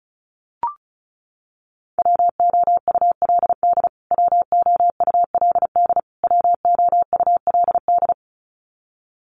Morse code, heard as a keyed tone, about 34 wpm, 700 Hz. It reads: WOULD WOULD WOULD
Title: Top 500 Words Repeated 35wpm